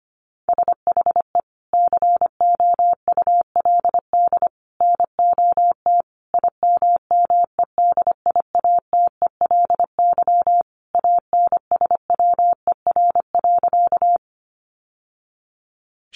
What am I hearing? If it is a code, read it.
Morse code, heard as a keyed tone, about 25 wpm, 700 Hz. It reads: SHE COULD NOT IMMEDIATELY ANSWER.
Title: Sentences from Top 1100 Words 25wpm